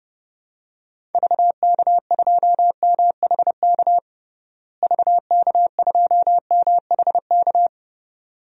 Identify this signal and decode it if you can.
Morse code, heard as a keyed tone, about 30 wpm, 700 Hz. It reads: VK2MHK VK2MHK
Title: DXCC-100 Call Signs - 6 Characters - Repeated 2x - 30wpm